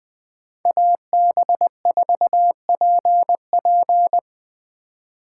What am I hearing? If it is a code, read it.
Morse code, heard as a keyed tone, about 21 wpm, 700 Hz. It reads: AB4PP
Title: 2023 SOTA Chaser Call Signs - 20wpm